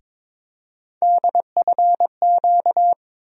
Morse code, heard as a keyed tone, about 22 wpm, 700 Hz. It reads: DFQ
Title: USA Call Sign Postfixes - 3 Letters 22wpm